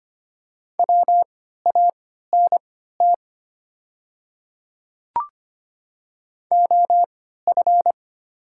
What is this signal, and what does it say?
Morse code, heard as a keyed tone, about 25 wpm, spaced as for 14 wpm, 700 Hz. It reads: WANT OF
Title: 3x Character Spacing - Top 300 Words - 25wpm